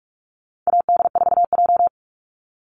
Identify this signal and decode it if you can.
Morse code, heard as a keyed tone, about 44 wpm, 700 Hz. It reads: AD4J
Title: Call Signs - 45wpm